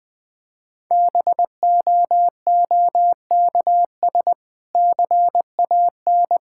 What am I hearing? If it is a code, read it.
Morse code, heard as a keyed tone, about 20 wpm, 700 Hz. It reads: BOOKS CAN
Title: Difficult Sentences with Hints - 20wpm